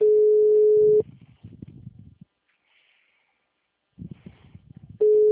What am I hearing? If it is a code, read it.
Ringback tone (Europe, 425 Hz).
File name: out- -2034- - - .80550